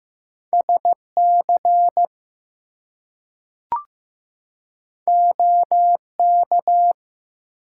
Morse code, heard as a keyed tone, about 15 wpm, 700 Hz. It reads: SC OK
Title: US States and Canadian Provinces 15wpm